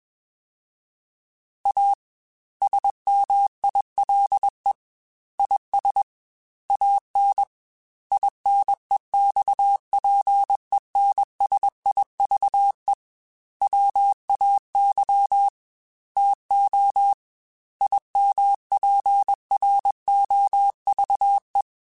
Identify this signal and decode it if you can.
Morse code, heard as a keyed tone, about 21 wpm, 800 Hz. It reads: A SMILE IS AN INEXPENSIVE WAY TO IMPROVE
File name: Quotes / Quotes-mp3s / A Smile @20